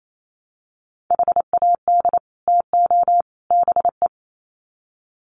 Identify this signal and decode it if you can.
Morse code, heard as a keyed tone, about 28 wpm, 700 Hz. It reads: HAD TO BE
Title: Sets of 3 Words 28wpm